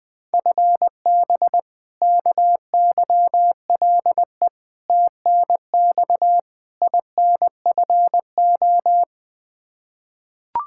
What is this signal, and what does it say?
Morse code, heard as a keyed tone, about 20 wpm, 700 Hz. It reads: FB KYLE TNX INFO
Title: Practice QSO - Exchange 3 - More Info - 20wpm